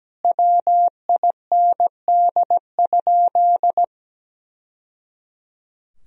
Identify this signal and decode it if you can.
Morse code, heard as a keyed tone, about 17 wpm, 700 Hz. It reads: WIND?
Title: Sentences from Top 1000 Words - Review Part 2 - 17wpm